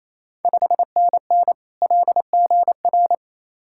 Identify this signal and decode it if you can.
Morse code, heard as a keyed tone, about 28 wpm, 700 Hz. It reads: 5NN LGR
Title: Ontario QSO Party Exchange 28wpm